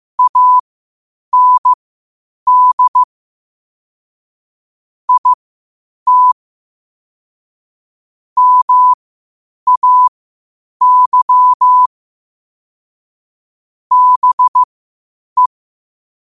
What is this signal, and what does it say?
Morse code, heard as a keyed tone, about 15 wpm, spaced as for 8 wpm, 1000 Hz. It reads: AND IT MAY BE
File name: qotdfiles / 8wpm